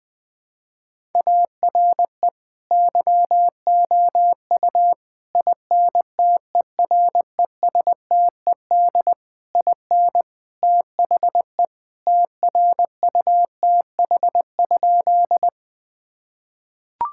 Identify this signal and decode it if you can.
Morse code, heard as a keyed tone, about 20 wpm, 700 Hz. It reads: ARE YOU INTERESTED IN THE TRUTH?
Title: Sentences from Top 1000 Words - Review Part 6 - 20wpm